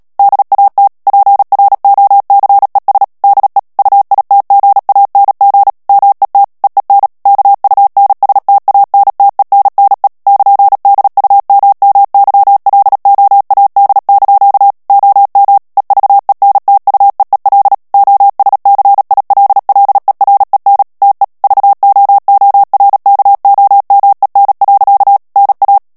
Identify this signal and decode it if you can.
Morse code, heard as a keyed tone, about 37 wpm, 800 Hz. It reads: DAT PROCES DE UITGANG MET EEN KUNSTANTENNE <KN>DUMMYLOAD) OM EVENTUEEL OSCILLEREN TE VOORKOMEN. NA
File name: morse mp3 / HOLLANDAIS texte clair 2 vitesse 35 mots minute